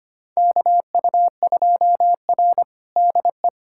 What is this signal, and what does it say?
Morse code, heard as a keyed tone, about 25 wpm, 700 Hz. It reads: KU2R DE
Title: USA Callsign DE Callsign 25wpm